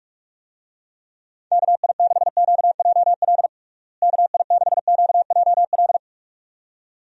Morse code, heard as a keyed tone, about 44 wpm, 700 Hz. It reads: KI6QJL KI6QJL
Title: USA Call Signs - 6 Characters - Repeated 2x - 45wpm